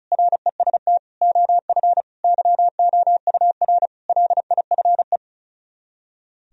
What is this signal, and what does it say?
Morse code, heard as a keyed tone, about 46 wpm, 700 Hz. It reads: REST OF YOUR LIFE